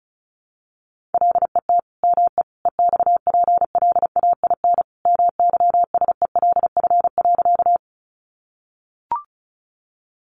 Morse code, heard as a keyed tone, about 34 wpm, 700 Hz. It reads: LET ME EXPLAIN MYSELF.
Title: Sentences from Top 1300 Words 35wpm